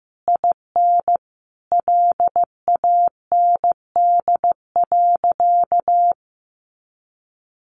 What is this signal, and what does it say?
Morse code, heard as a keyed tone, about 15 wpm, 700 Hz. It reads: IN LAND.
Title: Sentences from Top 1100 Words 15wpm